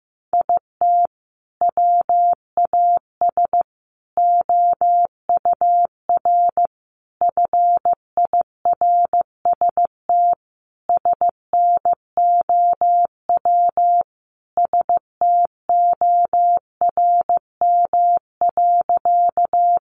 Morse code, heard as a keyed tone, about 15 wpm, 700 Hz. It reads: IT WAS OUR FIRST SNOW STORM.
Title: Sentences from Top 1500 Words 15wpm